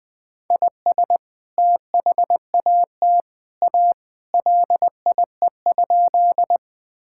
Morse code, heard as a keyed tone, about 21 wpm, 700 Hz. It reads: IS THAT A LIE?
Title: Sentences from Top 1600 Words 20wpm